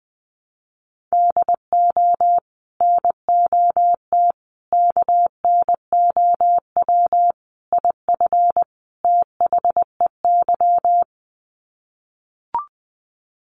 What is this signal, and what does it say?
Morse code, heard as a keyed tone, about 20 wpm, 700 Hz. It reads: DO NOT KNOW IF THEY
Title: Sets of 5 Words Encore 20wpm